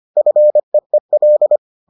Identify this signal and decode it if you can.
Morse code, heard as a keyed tone, about 25 wpm, 600 Hz. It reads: FEEL